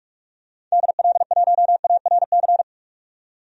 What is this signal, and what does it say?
Morse code, heard as a keyed tone, about 47 wpm, 700 Hz. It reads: DL1ARC
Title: DXCC-100 Call Signs - 6 Characters - 45wpm